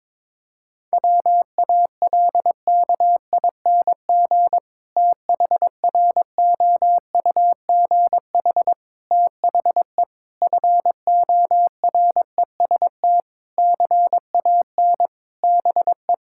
Morse code, heard as a keyed tone, about 22 wpm, 700 Hz. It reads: WALKING THROUGH THE FOREST CAN BE
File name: Difficult Sentences with Hints - 22wpm